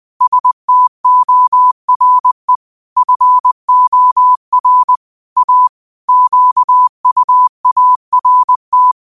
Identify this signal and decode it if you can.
Morse code, heard as a keyed tone, about 20 wpm, 1000 Hz. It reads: STORE FOR A QUART